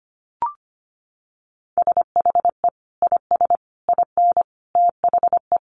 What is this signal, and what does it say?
Morse code, heard as a keyed tone, about 25 wpm, 700 Hz. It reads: SHE IS IN THE
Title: Sentences from Top 600 Words 25wpm